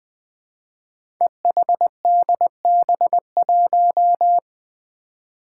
Morse code, heard as a keyed tone, about 20 wpm, 700 Hz. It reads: EHDB1
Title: Five Challenging Letter-Number Combinations 20wpm